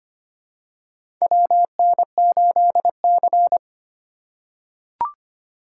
Morse code, heard as a keyed tone, about 25 wpm, 690 Hz. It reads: WN8C
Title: USA Call Signs - 4 Characters - 4x Character Spacing - 25wpm